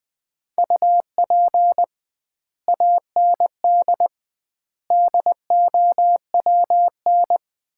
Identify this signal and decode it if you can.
Morse code, heard as a keyed tone, about 21 wpm, 700 Hz. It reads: UP AND DOWN
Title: 2x Word-Spacing - Sets of 4 Words - 20wpm